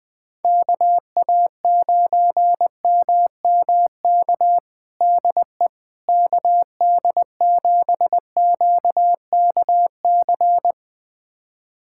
Morse code, heard as a keyed tone, about 21 wpm, 700 Hz. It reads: KA9MMK DE KD7QKC
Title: USA Callsign DE Callsign 20wpm